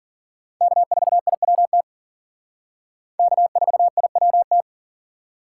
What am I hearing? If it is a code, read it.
Morse code, heard as a keyed tone, about 50 wpm, 700 Hz. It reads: K4IWT K4IWT
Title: Speed Racing - USA 1x3 Call Signs 40wpm